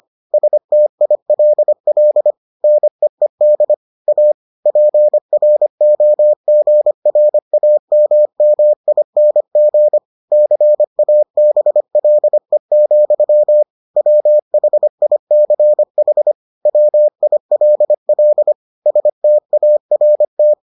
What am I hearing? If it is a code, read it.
Morse code, heard as a keyed tone, about 25 wpm, 600 Hz. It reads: STILL NEED A PROGRAMMING CABLE, WHICH WILL START